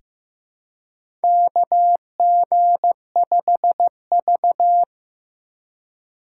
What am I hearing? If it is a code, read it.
Morse code, heard as a keyed tone, about 15 wpm, 700 Hz. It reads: KG5V
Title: K1USN SST QSO Practice - 15wpm